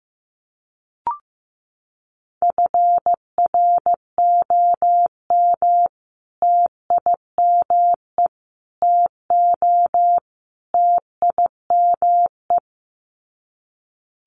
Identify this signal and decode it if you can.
Morse code, heard as a keyed tone, about 15 wpm, 700 Hz. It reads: FROM TIME TO TIME
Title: Sets of 4 Words 15wpm